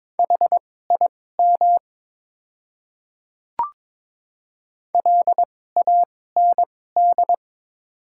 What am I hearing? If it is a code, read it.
Morse code, heard as a keyed tone, about 23 wpm, spaced as for 16 wpm, 700 Hz. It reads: HIM LAND
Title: 2x Character Spacing - Top 200 Words - 22wpm